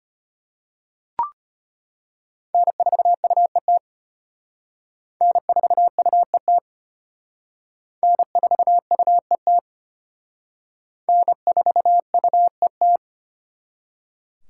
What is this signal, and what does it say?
Morse code, heard as a keyed tone, about 33 wpm, 700 Hz. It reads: N4UET N4UET N4UET N4UET